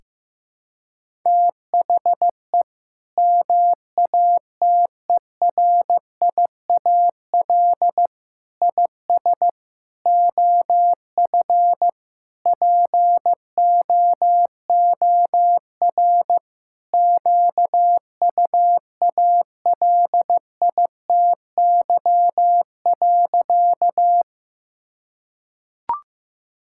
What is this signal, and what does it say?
Morse code, heard as a keyed tone, about 15 wpm, 700 Hz. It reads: THE MATERIAL IS OF POOR QUALITY.
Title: Sentences from Top 1000 Words - Review Part 5 - 15wpm